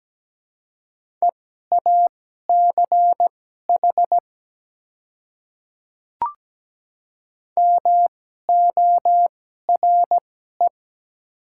Morse code, heard as a keyed tone, about 17 wpm, spaced as for 12 wpm, 700 Hz. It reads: EACH MORE
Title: 2x Character Spacing - Top 100 Words - 17wpm